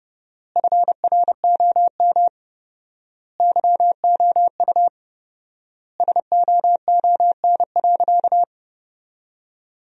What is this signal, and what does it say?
Morse code, heard as a keyed tone, about 30 wpm, 700 Hz. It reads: FROM YOU SOON.
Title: 4x Word-Spacing - Sentences from Top 400 Words - 30wpm